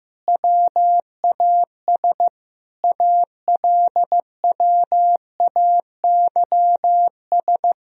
Morse code, heard as a keyed tone, about 15 wpm, 700 Hz. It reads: WAS ALWAYS
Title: Sentences from Top 1000 Words - Review Part 1 - 15wpm